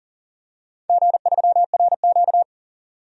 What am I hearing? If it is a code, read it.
Morse code, heard as a keyed tone, about 43 wpm, 700 Hz. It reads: G3RQ
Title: DXCC-100 Call Signs - 4 Characters - 40wpm